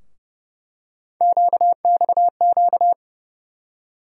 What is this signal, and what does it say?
Morse code, heard as a keyed tone, about 32 wpm, 700 Hz. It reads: QXQ
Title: USA Call Sign Postfixes - 3 Letters 30wpm